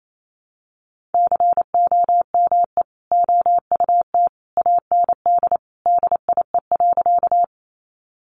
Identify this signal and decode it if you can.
Morse code, heard as a keyed tone, about 28 wpm, 700 Hz. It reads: COME OUT AND DIE.
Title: Sentences from Top 1300 Words 28wpm